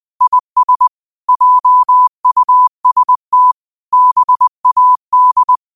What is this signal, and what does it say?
Morse code, heard as a keyed tone, about 20 wpm, 1000 Hz. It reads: IS JUST BAD
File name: qotdfiles / 20wpm